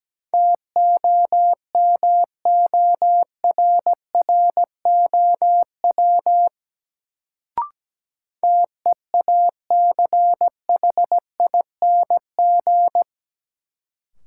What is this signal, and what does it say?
Morse code, heard as a keyed tone, about 17 wpm, 700 Hz. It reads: TOMORROW TEACHING
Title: Sentences from Top 1100 Words - New Word Drill - 17wpm